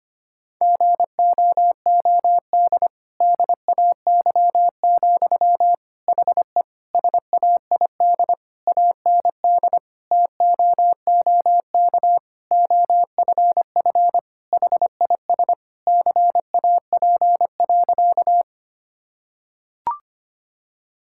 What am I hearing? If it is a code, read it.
Morse code, heard as a keyed tone, about 25 wpm, 700 Hz. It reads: GOOD DAY, HE SAID AND TOOK OFF HIS CAP.
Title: Sentences from Top 1600 Words 25wpm